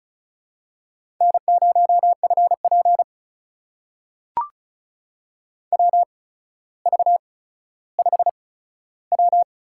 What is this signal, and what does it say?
Morse code, heard as a keyed tone, about 36 wpm, spaced as for 15 wpm, 700 Hz. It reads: N0FP WV5W
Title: USA Call Signs - 4 Characters - 8x Character Spacing - 35wpm